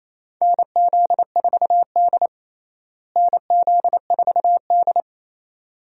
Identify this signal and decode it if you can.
Morse code, heard as a keyed tone, about 28 wpm, 700 Hz. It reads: NZ4D NZ4D
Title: USA Call Signs - 4 Characters - Repeated 2x - 28wpm